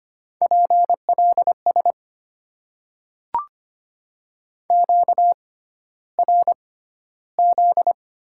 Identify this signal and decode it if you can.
Morse code, heard as a keyed tone, about 25 wpm, spaced as for 13 wpm, 700 Hz. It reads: PLS QRZ
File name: Top 90 QSO Elements - 6x Character Spacing - 25wpm